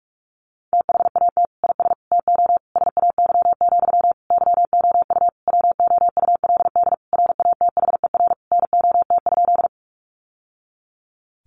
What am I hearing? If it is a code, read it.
Morse code, heard as a keyed tone, about 44 wpm, 700 Hz. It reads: THAT IS TO SAY, YOU WOULD RATHER NOT?
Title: Sentences from Top 700 Words 45wpm